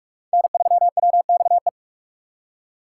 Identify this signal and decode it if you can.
Morse code, heard as a keyed tone, about 41 wpm, 700 Hz. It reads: N3WXE
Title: USA Call Signs - 5 Characters - 2x Character Spacing - 45wpm